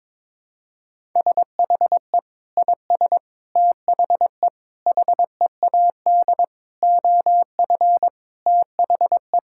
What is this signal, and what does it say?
Morse code, heard as a keyed tone, about 21 wpm, 700 Hz. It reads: SHE IS THE HEAD OF THE
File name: practice / 22wpm / Sentences with 3-Word Phrases - 22wpm